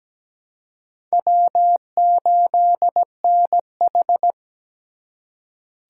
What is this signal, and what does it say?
Morse code, heard as a keyed tone, about 17 wpm, 700 Hz. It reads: W8NH
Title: USA Call Signs - 4 Characters - 6x Character Spacing - 17wpm